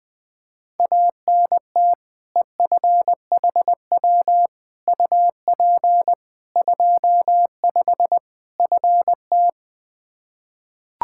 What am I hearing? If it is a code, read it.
Morse code, heard as a keyed tone, about 21 wpm, 700 Hz. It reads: ANT EFHW UP 25 FT
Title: K1USN SST QSO Practice 20wpm